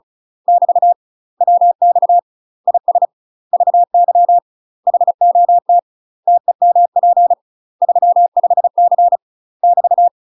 Morse code, heard as a keyed tone, about 34 wpm, 700 Hz. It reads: <BT> WX IS VY HOT TEMP 35C <BT>
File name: UK MASTER QSOs TRAINING / 35WPM / UK MASTER QSOs 0040